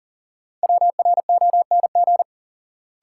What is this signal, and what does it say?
Morse code, heard as a keyed tone, about 40 wpm, 700 Hz. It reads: WRONG
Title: Sentences from Top 900 Words - New Word Drill - 40wpm